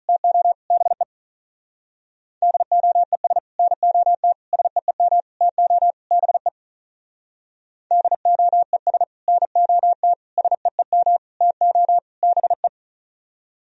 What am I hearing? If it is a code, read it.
Morse code, heard as a keyed tone, about 44 wpm, 700 Hz. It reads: TO BE DOES NOT SEEM TO BE DOES NOT SEEM TO BE